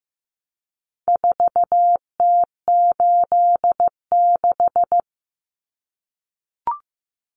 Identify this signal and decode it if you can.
Morse code, heard as a keyed tone, about 15 wpm, 700 Hz. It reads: 4T86
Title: Common Cut Numbers - 4-Digit 15wpm